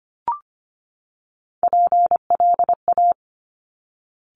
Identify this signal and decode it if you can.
Morse code, heard as a keyed tone, about 25 wpm, 700 Hz. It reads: PLA
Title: Top Three Letter Combinations in Words 25wpm